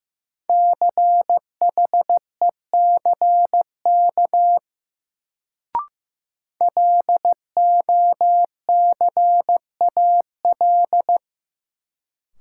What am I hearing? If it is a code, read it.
Morse code, heard as a keyed tone, about 15 wpm, 700 Hz. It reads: CHECK LOCAL
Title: Sentences from Top 1000 Words - Review Part 3 - New Word Drill - 15wpm